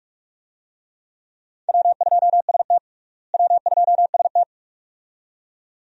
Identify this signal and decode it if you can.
Morse code, heard as a keyed tone, about 46 wpm, 700 Hz. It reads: W2ST W2ST
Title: USA Call Signs - 4 Characters - Repeated 2x - 45wpm